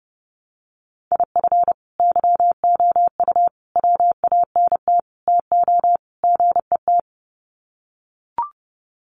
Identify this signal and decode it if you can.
Morse code, heard as a keyed tone, about 29 wpm, 700 Hz. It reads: IF YOU WANT TO GET
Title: Sets of 5 Words Encore 30wpm